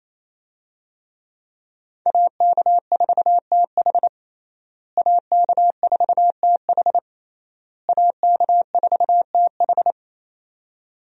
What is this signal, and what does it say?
Morse code, heard as a keyed tone, about 28 wpm, 700 Hz. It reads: AK4TH AK4TH AK4TH
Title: USA Call Signs - 5 Characters - Repeated 3x - 28wpm